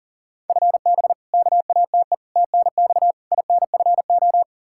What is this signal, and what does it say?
Morse code, heard as a keyed tone, about 37 wpm, 700 Hz. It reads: FB KATE TNX INFO
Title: Practice QSO - Exchange 3 - More Info - 40wpm